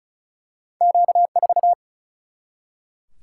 Morse code, heard as a keyed tone, about 34 wpm, 700 Hz. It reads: Q4
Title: Single-Letter Single-Number 35wpm